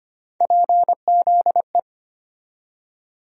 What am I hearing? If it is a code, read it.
Morse code, heard as a keyed tone, about 25 wpm, 700 Hz. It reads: PZE
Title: USA Call Sign Postfixes - 3 Letters 25wpm